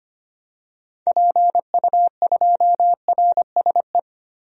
Morse code, heard as a keyed tone, about 25 wpm, 700 Hz. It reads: PU2RSE
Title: DXCC-100 Call Signs - 6 Characters - 25wpm